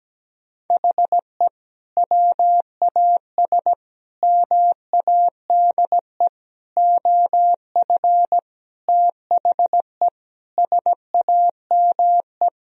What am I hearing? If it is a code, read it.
Morse code, heard as a keyed tone, about 17 wpm, 700 Hz. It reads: HE WAS MADE OF THE SAME
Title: Sentences from Top 1000 Words - Review Part 4 - 17wpm